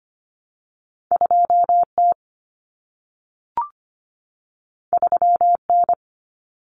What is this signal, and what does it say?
Morse code, heard as a keyed tone, about 25 wpm, 700 Hz. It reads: 2T 3N
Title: Common Cut Numbers - 2-Digit 25wpm